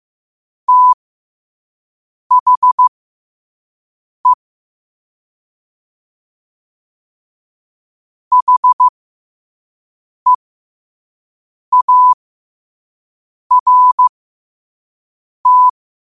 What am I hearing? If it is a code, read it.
Morse code, heard as a keyed tone, about 15 wpm, spaced as for 5 wpm, 1000 Hz. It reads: THE HEART
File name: qotdfiles / 5wpm